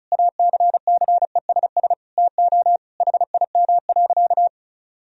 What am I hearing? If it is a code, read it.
Morse code, heard as a keyed tone, about 36 wpm, 700 Hz. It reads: ACCESS TO HIM.
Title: Sentences from Top 1000 Words - Review Part 4 - 35wpm